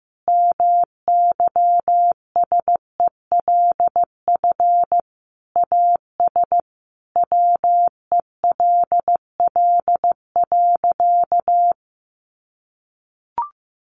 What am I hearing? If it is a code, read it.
Morse code, heard as a keyed tone, about 15 wpm, 700 Hz. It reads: MYSELF AS WELL.
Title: Sentences from Top 1300 Words 15wpm